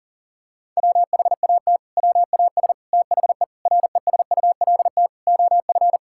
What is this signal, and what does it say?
Morse code, heard as a keyed tone, about 40 wpm, 700 Hz. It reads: WHAT WAS THE RESULT OF
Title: Sentences from Top 1000 Words - Review Part 5 - 40wpm